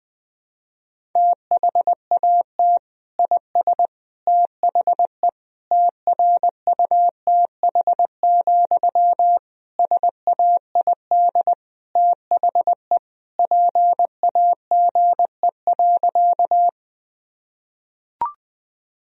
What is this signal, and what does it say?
Morse code, heard as a keyed tone, about 21 wpm, 700 Hz. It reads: THAT IS THE TRUTH, SAID THE PAGE.
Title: Sentences from Top 1000 Words - Review Part 1 - 20wpm